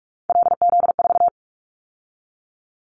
Morse code, heard as a keyed tone, about 46 wpm, 700 Hz. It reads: LZ4
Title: Callsign Prefixes for Top 50 DXCC - 45wpm